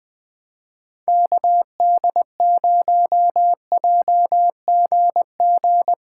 Morse code, heard as a keyed tone, about 20 wpm, 700 Hz. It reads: KD0JGG
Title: USA Callsign DE Callsign 20wpm